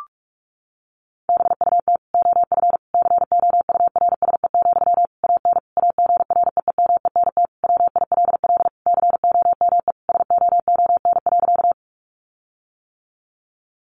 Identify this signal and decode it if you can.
Morse code, heard as a keyed tone, about 44 wpm, 700 Hz. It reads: BUT OF COURSE, AN AGREEMENT WILL COME SOON.
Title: Sentences from Top 1500 Words 45wpm